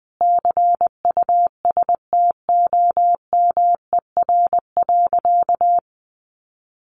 Morse code, heard as a keyed tone, about 20 wpm, 700 Hz. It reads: CUSTOMER.
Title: Sentences from Top 1100 Words 20wpm